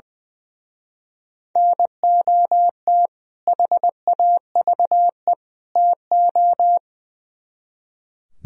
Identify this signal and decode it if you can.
Morse code, heard as a keyed tone, about 21 wpm, 700 Hz. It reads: NOT HAVE TO
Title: Speed-Racing - Sets of 3 Words 20wpm